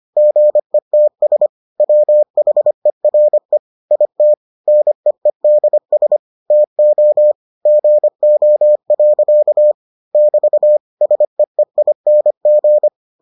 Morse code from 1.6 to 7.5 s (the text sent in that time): WHERE IT NEEDS TO